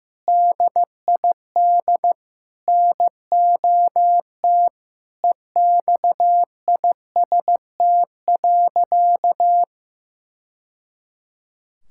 Morse code, heard as a keyed tone, about 15 wpm, 700 Hz. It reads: DID NOT EXIST.
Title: Sentences from Top 1000 Words - Review Part 6 - 15wpm